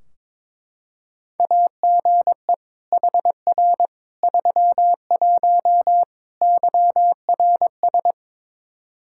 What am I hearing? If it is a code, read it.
Morse code, heard as a keyed tone, about 22 wpm, 700 Hz. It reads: AGE HR 31 YRS
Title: Practice QSO - Exchange 3 - More Info - 22wpm